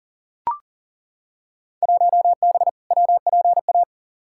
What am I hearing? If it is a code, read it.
Morse code, heard as a keyed tone, about 37 wpm, 700 Hz. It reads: 1B WPA
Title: Field Day Exchanges - 40wpm